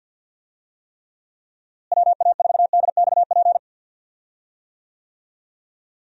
Morse code, heard as a keyed tone, about 52 wpm, 700 Hz. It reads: WA4DXP
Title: USA Call Signs - 6 Characters - Repeated 3x - 50wpm